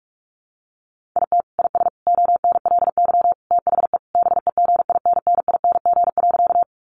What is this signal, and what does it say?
Morse code, heard as a keyed tone, about 44 wpm, 700 Hz. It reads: IT IS ONLY THE BEGINNING.
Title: Sentences from Top 800 Words 45wpm